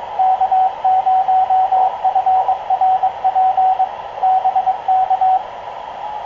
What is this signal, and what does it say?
Morse code, heard as a keyed tone, about 22 wpm, 750 Hz, with radio noise. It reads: K0FRP BK
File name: k0frp ki0ii wa8bxn fox utc